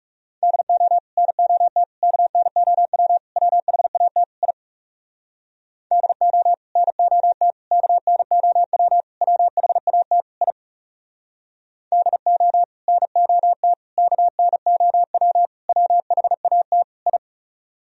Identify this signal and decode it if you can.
Morse code, heard as a keyed tone, about 40 wpm, 700 Hz. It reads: DO NOT KNOW WHAT I DO NOT KNOW WHAT I DO NOT KNOW WHAT I